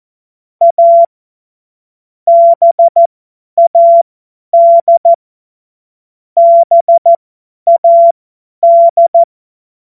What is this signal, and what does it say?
Morse code, heard as a keyed tone, about 14 wpm, spaced as for 10 wpm, 680 Hz. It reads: A BAD BAD
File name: Morsekurs / Witze / Witz 14. Dezember 2021, 50 F70